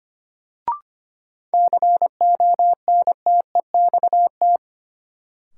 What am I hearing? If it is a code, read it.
Morse code, heard as a keyed tone, about 25 wpm, 700 Hz. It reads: CONTEXT
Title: Sentences from Top 1000 Words - Review Part 7 - New Word Drill - 25wpm